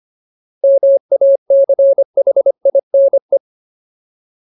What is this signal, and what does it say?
Morse code, heard as a keyed tone, about 25 wpm, 550 Hz. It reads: MACHINE
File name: Ultimate Word List / 25WPM Ultimate